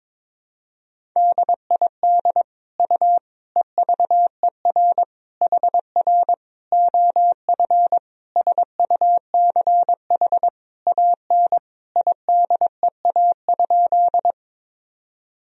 Morse code, heard as a keyed tone, about 22 wpm, 700 Hz. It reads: DID U EVER HR OF SUCH AN IDEA?
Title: Sentences from Top 400 Words and Abbreviations 22wpm